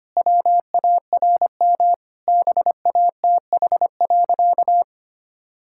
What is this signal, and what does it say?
Morse code, heard as a keyed tone, about 25 wpm, 700 Hz. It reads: WARM BATH.
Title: Sentences from Top 1600 Words 25wpm